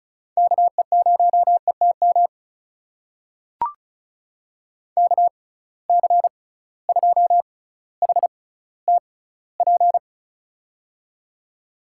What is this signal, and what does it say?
Morse code, heard as a keyed tone, about 36 wpm, spaced as for 18 wpm, 700 Hz. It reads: KE0ETM KC2HTP
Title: USA Call Signs - 6 Characters - 6x Character Spacing - 35wpm